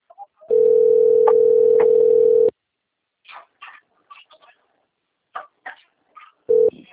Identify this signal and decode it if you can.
Ringback tone (North America), 440 + 480 Hz.